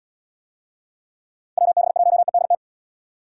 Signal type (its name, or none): Morse code